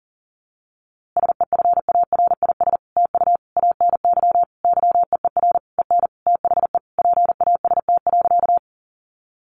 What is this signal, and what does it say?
Morse code, heard as a keyed tone, about 40 wpm, 700 Hz. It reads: SEFARIS TU ANY YEER EN THE PAST.
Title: A Sound Of Thunder 40wpm